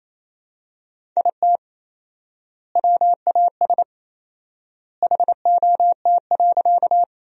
Morse code, heard as a keyed tone, about 28 wpm, 700 Hz. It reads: IT WAS HOT.
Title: 4x Word-Spacing - Sentences from Top 500 Words - 28wpm